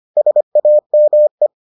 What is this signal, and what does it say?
Morse code, heard as a keyed tone, about 25 wpm, 600 Hz. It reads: SAME